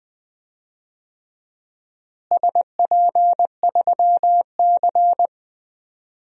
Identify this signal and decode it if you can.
Morse code, heard as a keyed tone, about 20 wpm, 700 Hz. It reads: SP3C
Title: DXCC-100 Call Signs - 4 Characters - Repeated 3x - 20wpm